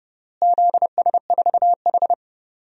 Morse code, heard as a keyed tone, about 30 wpm, 700 Hz. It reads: ZS4H
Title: DXCC-100 Call Signs - 4 Characters - Repeated 3x - 30wpm